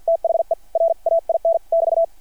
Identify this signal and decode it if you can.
Morse code, heard as a keyed tone, about 50 wpm, spaced as for 43 wpm, 660 Hz. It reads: THE RAIN <BT>
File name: archived / CRSnet